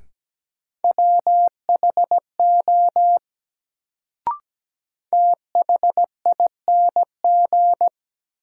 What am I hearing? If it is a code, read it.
Morse code, heard as a keyed tone, about 17 wpm, 700 Hz. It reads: WHO THING
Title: Sentences from Top 100 Words - New Word Drill - 17wpm